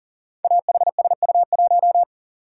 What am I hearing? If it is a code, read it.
Morse code, heard as a keyed tone, about 41 wpm, 690 Hz. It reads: AHSU1
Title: Code Groups - 5 Letter:Numbers - 40wpm